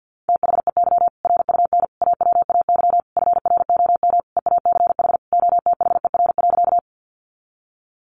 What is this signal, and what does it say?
Morse code, heard as a keyed tone, about 50 wpm, 700 Hz. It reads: THEY RUN AWAY FROM EACH OTHER.
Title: Easy Phrases 50wpm